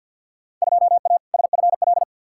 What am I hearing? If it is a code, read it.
Morse code, heard as a keyed tone, about 52 wpm, 700 Hz. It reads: 2A SFL
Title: Field Day Exchanges Encore - 50wpm